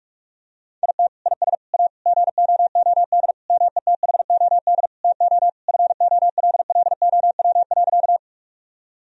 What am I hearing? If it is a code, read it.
Morse code, heard as a keyed tone, about 46 wpm, 700 Hz. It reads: IT IS A GOOD METHOD TO FOLLOW.
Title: Sentences from Top 1000 Words - Review Part 3 - 45wpm